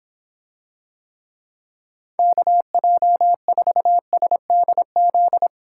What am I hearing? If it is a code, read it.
Morse code, heard as a keyed tone, about 26 wpm, 700 Hz. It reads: KJ4SDZ